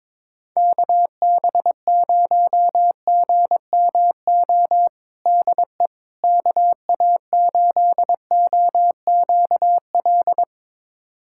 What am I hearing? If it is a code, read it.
Morse code, heard as a keyed tone, about 22 wpm, 700 Hz. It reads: KB0GMO DE KA8OQL
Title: USA Callsign DE Callsign 22wpm